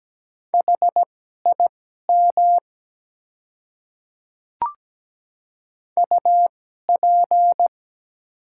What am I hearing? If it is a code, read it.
Morse code, heard as a keyed tone, about 17 wpm, spaced as for 12 wpm, 700 Hz. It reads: HIM UP
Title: 2x Character Spacing - Top 100 Words - 17wpm